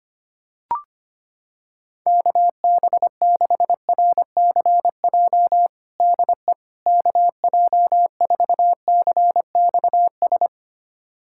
Morse code, heard as a keyed tone, about 25 wpm, 700 Hz. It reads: KB6RCJ DE KJ4CXS
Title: USA Callsign DE Callsign - 6 Characters - 25wpm